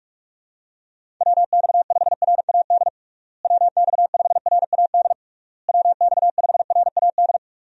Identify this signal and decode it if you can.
Morse code, heard as a keyed tone, about 46 wpm, 700 Hz. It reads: WX5RAD WX5RAD WX5RAD
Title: USA Call Signs - 6 Characters - Repeated 3x - 45wpm